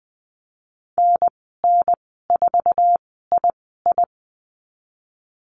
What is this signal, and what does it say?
Morse code, heard as a keyed tone, about 20 wpm, spaced as for 15 wpm, 700 Hz. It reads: NN4II
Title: 076 Lesson 13g - Letter L - Call Signs 20wpm